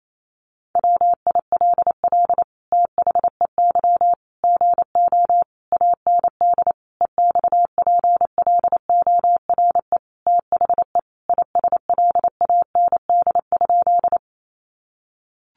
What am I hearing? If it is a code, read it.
Morse code, heard as a keyed tone, about 28 wpm, 700 Hz. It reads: WILL THEY GO AND EXPLORE THE ISLAND?
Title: Sentences from Top 1100 Words 28wpm